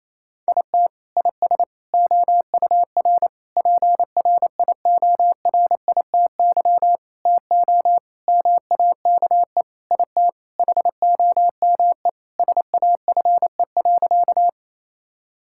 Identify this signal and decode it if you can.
Morse code, heard as a keyed tone, about 28 wpm, 700 Hz. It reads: IT IS OUR PRIORITY TO MAKE IT HOME SAFE.
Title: Sentences from Top 1600 Words 28wpm